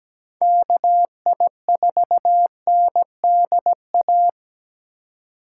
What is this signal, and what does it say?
Morse code, heard as a keyed tone, about 17 wpm, 700 Hz. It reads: KI4NDA
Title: USA Call Signs - 3x Character Spacing - 17wpm